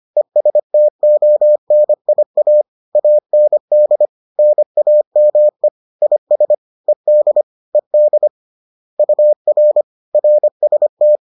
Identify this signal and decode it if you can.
Morse code, heard as a keyed tone, about 25 wpm, 600 Hz. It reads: ESTONIA AND NAME IS ED ED UR RST
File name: 25wpm-6